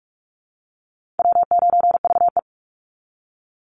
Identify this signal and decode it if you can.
Morse code, heard as a keyed tone, about 44 wpm, 700 Hz. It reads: W9VE
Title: Easy Call Signs Repeated 2x - 45wpm